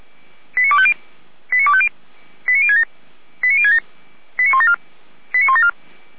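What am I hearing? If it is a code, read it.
Five-tone selective call (ZVEI), heard as quick runs of five tones, tones of about 70 ms. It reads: 89250 89250 89067 89067 89174 89174
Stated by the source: ZVEI five-tone call 89240 received on SDR by feuerwehr-orlamuende.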